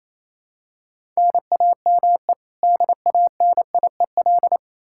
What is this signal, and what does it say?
Morse code, heard as a keyed tone, about 28 wpm, 700 Hz. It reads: NAME DANIEL
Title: Practice QSO - Exchange 1 - Beginning - 28wpm